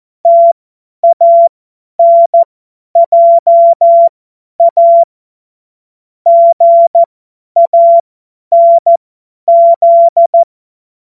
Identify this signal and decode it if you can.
Morse code, heard as a keyed tone, about 14 wpm, spaced as for 10 wpm, 680 Hz. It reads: TANJA GANZ
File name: Morsekurs / Witze / Witz 07. März 2023 50 F70